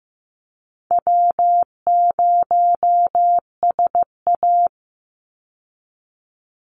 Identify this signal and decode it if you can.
Morse code, heard as a keyed tone, about 15 wpm, 700 Hz. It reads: W0SA
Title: Call Signs Repeated 3x - 15wpm